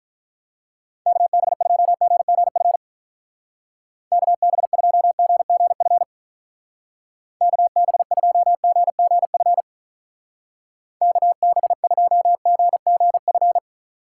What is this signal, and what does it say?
Morse code, heard as a keyed tone, about 43 wpm, 700 Hz. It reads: KB2GGF KB2GGF KB2GGF KB2GGF
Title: Speed Racing - USA 2x3 Call Signs 35wpm